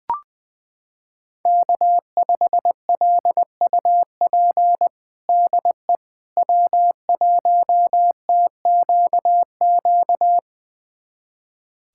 Morse code, heard as a keyed tone, about 20 wpm, 700 Hz. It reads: K5LUP DE W1TQQ
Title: USA Callsign DE Callsign - 5 Characters - 20wpm